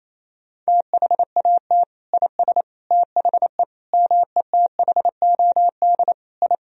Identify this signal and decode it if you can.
Morse code, heard as a keyed tone, about 28 wpm, 700 Hz. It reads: THAT IS THE METHOD I
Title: Sentences from Top 1000 Words - Review Part 3 - 28wpm